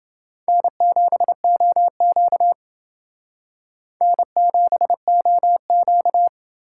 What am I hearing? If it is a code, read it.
Morse code, heard as a keyed tone, about 30 wpm, 700 Hz. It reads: N7OQ N7OQ